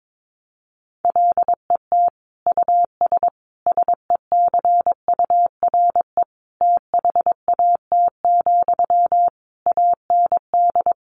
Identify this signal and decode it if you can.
Morse code, heard as a keyed tone, about 22 wpm, 700 Hz. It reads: LET US SECURE THAT, AND
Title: Sentences from Top 1200 Words 22wpm